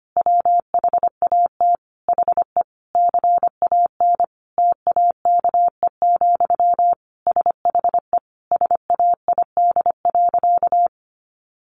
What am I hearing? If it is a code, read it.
Morse code, heard as a keyed tone, about 25 wpm, 700 Hz. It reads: WHAT HE CAN TAKE, SHE SAID.
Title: Sentences from Top 1500 Words 25wpm